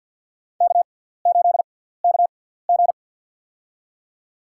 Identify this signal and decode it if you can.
Morse code, heard as a keyed tone, about 48 wpm, spaced as for 17 wpm, 700 Hz. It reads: K8KG
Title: USA Call Signs - 4 Characters - 6x Character Spacing - 50wpm